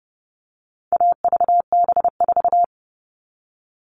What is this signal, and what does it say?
Morse code, heard as a keyed tone, about 30 wpm, 700 Hz. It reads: AVB4
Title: All Cut Numbers - 4-Digit 30wpm